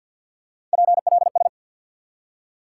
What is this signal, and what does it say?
Morse code, heard as a keyed tone, about 50 wpm, 700 Hz. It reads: PLS
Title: Top 180 QSO Elements 50wpm